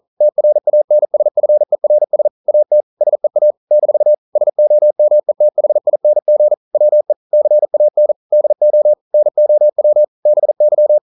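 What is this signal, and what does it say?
Morse code, heard as a keyed tone, about 42 wpm, 600 Hz. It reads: TRANSFERS AT SEA - SOMETHING WE CAN DO NOW BY
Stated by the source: Morse code practice at 41 wpm, from CW med Gnister.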